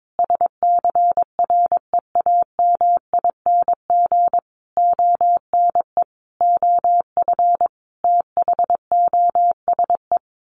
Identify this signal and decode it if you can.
Morse code, heard as a keyed tone, about 22 wpm, 700 Hz. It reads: SCREAMING ONE OF THOSE
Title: A Sound Of Thunder 22wpm